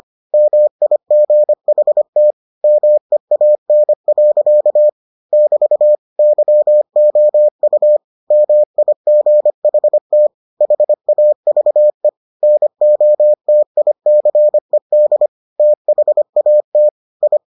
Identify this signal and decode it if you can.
Morse code, heard as a keyed tone, about 25 wpm, 600 Hz. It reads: MIGHT MEAN. <BT> YOU MIGHT HAVE NOTICED THAT I